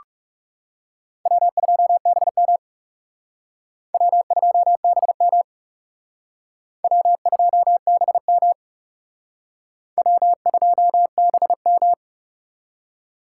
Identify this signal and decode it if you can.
Morse code, heard as a keyed tone, about 36 wpm, 700 Hz. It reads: W2BM W2BM W2BM W2BM
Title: Speed Racing - USA 1x2 Call Signs 30wpm